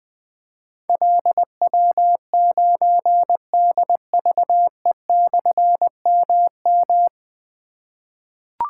Morse code, heard as a keyed tone, about 20 wpm, 700 Hz. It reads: LW9DVE/MM
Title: Difficult Call Signs 20wpm